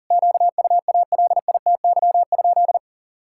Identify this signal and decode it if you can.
Morse code, heard as a keyed tone, about 40 wpm, 700 Hz. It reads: QUALITY?
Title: Sentences from Top 1000 Words - Review Part 6 - 40wpm